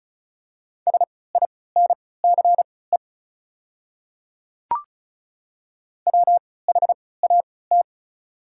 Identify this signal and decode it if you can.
Morse code, heard as a keyed tone, about 36 wpm, spaced as for 20 wpm, 690 Hz. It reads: SINCE WHAT
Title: 3x Character Spacing - Top 300 Words - 35wpm